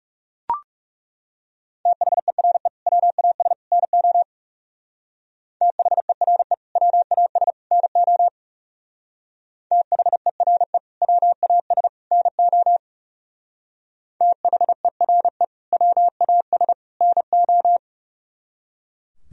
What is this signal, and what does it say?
Morse code, heard as a keyed tone, about 37 wpm, 700 Hz. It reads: THERE WAS NO THERE WAS NO THERE WAS NO THERE WAS NO
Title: Speed-Racing - Sets of 3 Words 30wpm